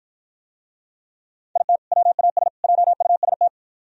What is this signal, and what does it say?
Morse code, heard as a keyed tone, about 52 wpm, 700 Hz. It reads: IT WAS JUST